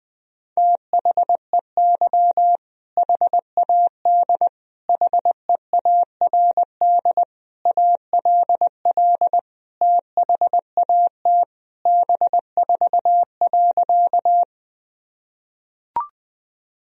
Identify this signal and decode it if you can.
Morse code, heard as a keyed tone, about 21 wpm, 700 Hz. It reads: THEY HAD HEARD ALL THAT B4.
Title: Sentences from Top 400 Words and Abbreviations 20wpm